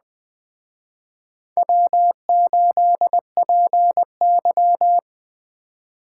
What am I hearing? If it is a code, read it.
Morse code, heard as a keyed tone, about 20 wpm, 700 Hz. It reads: W8PY
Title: Speed Racing - USA 1x2 Call Signs 17wpm